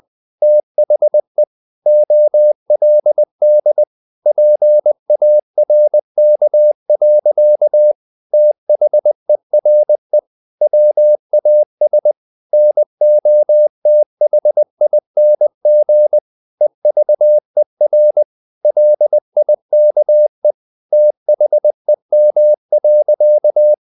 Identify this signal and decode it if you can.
Morse code, heard as a keyed tone, about 20 wpm, 600 Hz. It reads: THE OLD PARK. THERE WAS NOTHING EVER LIKE THEM.
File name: the old man and the sea / the old man and the sea part 1 20wpm